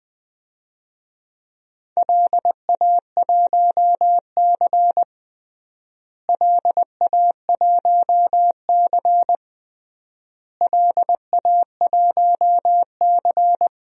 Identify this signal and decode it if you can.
Morse code, heard as a keyed tone, about 20 wpm, 700 Hz. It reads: LA1C LA1C LA1C
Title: DXCC-100 Call Signs - 4 Characters - Repeated 3x - 20wpm